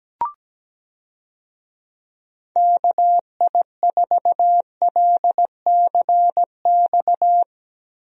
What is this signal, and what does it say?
Morse code, heard as a keyed tone, about 17 wpm, 700 Hz. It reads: KI4LCX
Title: USA Call Signs - 6 Characters - Repeated 2x - 17wpm